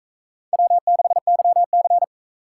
Morse code, heard as a keyed tone, about 41 wpm, 700 Hz. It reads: W6YC